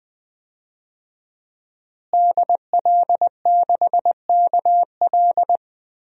Morse code, heard as a keyed tone, about 20 wpm, 700 Hz. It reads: DL6KL
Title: DXCC-100 Call Signs - 5 Characters - Repeated 3x - 20wpm